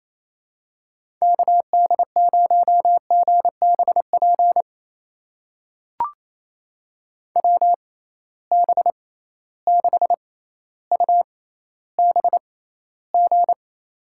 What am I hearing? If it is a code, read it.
Morse code, heard as a keyed tone, about 28 wpm, spaced as for 14 wpm, 700 Hz. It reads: KD0GBP WB6UBG
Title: USA Call Signs - 6x Character Spacing - 28wpm